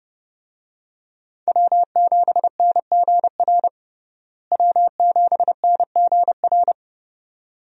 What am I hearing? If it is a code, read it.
Morse code, heard as a keyed tone, about 30 wpm, 700 Hz. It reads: W7NGR W7NGR
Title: USA Call Signs - 5 Characters - Repeated 2x - 30wpm